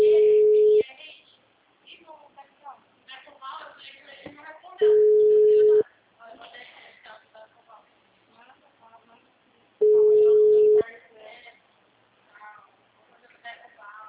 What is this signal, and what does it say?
Ringback tone (Europe, 425 Hz), with speech.